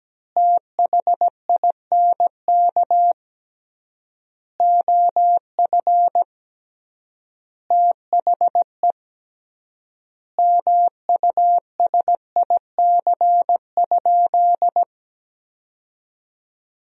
Morse code, heard as a keyed tone, about 17 wpm, 700 Hz. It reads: THINK OF THE MUSIC?
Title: 3x Word-Spacing - Sentences from Top 300 Words - 17wpm